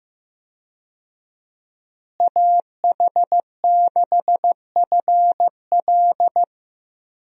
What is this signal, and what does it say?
Morse code, heard as a keyed tone, about 15 wpm, 700 Hz. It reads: AH6FL
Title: USA Call Signs - 5 Characters - Repeated 3x - 15wpm